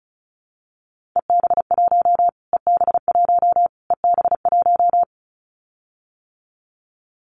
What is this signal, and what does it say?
Morse code, heard as a keyed tone, about 34 wpm, 700 Hz. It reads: EB1 EB1 EB1
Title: Callsign Prefixes for Top 25 DXCC - Repeated 3x - 35wpm